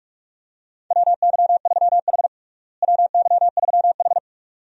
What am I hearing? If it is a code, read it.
Morse code, heard as a keyed tone, about 42 wpm, 700 Hz. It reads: WY3H WY3H
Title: USA Call Signs - 4 Characters - Repeated 2x - 45wpm